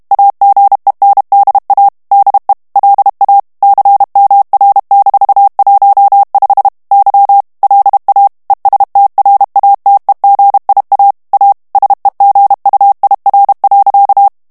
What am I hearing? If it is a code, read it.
Morse code, heard as a keyed tone, about 32 wpm, 800 Hz. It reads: AGENDA DE LA CMR-15 Y LA ESTRATEGIA A SEGUIR.
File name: morse mp3 / ESPAGNOL TEXTE EN CLAIR 1 VITESSE 30 MOTS MINUTES